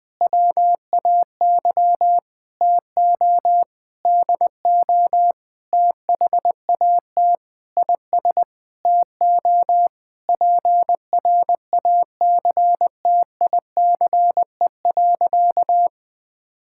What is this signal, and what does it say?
Morse code, heard as a keyed tone, about 20 wpm, 700 Hz. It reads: WAY TO DO THAT IS TO PRACTICE.
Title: Sentences with 8-Word Phrases - 20wpm